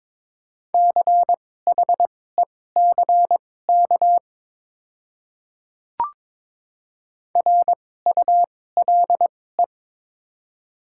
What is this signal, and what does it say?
Morse code, heard as a keyed tone, about 22 wpm, spaced as for 16 wpm, 690 Hz. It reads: CHECK RULE
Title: 2x Character Spacing - Top 500 Words - 22wpm